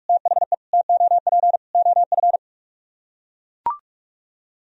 Morse code, heard as a keyed tone, about 42 wpm, 700 Hz. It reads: THE TOP OF